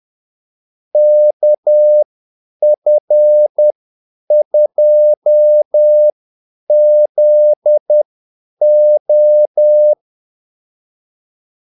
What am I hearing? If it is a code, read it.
Morse code, heard as a keyed tone, about 10 wpm, spaced as for 8 wpm, 600 Hz. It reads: KF2ZO
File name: Callsigns-10-8wpm